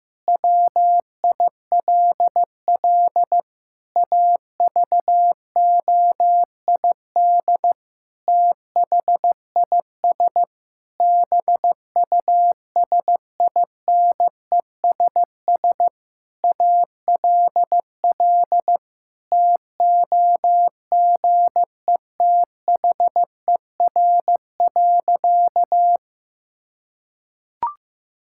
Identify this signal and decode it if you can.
Morse code, heard as a keyed tone, about 16 wpm, 700 Hz. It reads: WILL AVOID THIS BUSINESS ALL TOGETHER.
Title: Sentences from Top 1000 Words - Review Part 6 - 15wpm